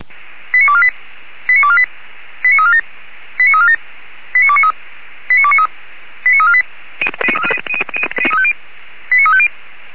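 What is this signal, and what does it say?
Five-tone selective call (ZVEI), heard as quick runs of five tones, tones of about 70 ms. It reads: 89248 89248 89347 89347 89293 89293 89348 89350
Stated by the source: Five-tone ZVEI selective call received by SDR, from the audio server of Feuerwehr Orlamünde.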